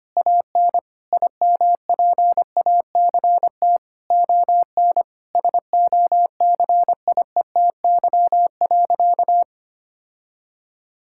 Morse code, heard as a keyed tone, about 25 wpm, 700 Hz. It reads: AN IMPACT ON SOCIETY.
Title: Sentences from Top 1000 Words - Review Part 6 - 25wpm